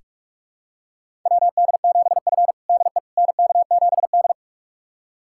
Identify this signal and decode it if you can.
Morse code, heard as a keyed tone, about 46 wpm, 700 Hz. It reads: WD7F DE NK7D
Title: USA Callsign DE Callsign - 4 Characters - 45wpm